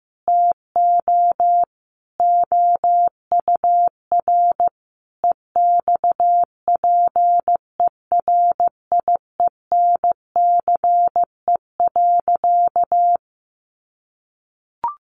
Morse code, heard as a keyed tone, about 14 wpm, 700 Hz. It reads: TO OUR EXPERIENCE.
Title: Sentences from Top 1200 Words 15wpm